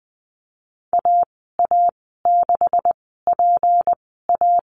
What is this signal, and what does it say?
Morse code, heard as a keyed tone, about 20 wpm, spaced as for 14 wpm, 700 Hz. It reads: AA6PA
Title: 161 Lesson 26g - Letter g - Call Signs 20wpm